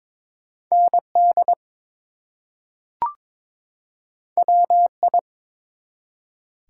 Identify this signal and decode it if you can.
Morse code, heard as a keyed tone, about 22 wpm, 700 Hz. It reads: ND WI
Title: US States and Canadian Provinces 22wpm